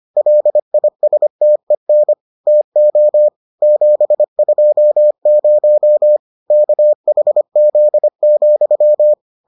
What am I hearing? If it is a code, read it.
Morse code, heard as a keyed tone, about 25 wpm, 600 Hz. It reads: LISTEN TO 720 KHZ,